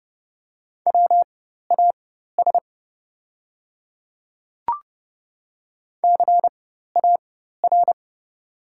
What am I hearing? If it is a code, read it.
Morse code, heard as a keyed tone, about 30 wpm, spaced as for 14 wpm, 690 Hz. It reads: WAS CAR
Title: 4x Character Spacing - Top 300 Words - 30wpm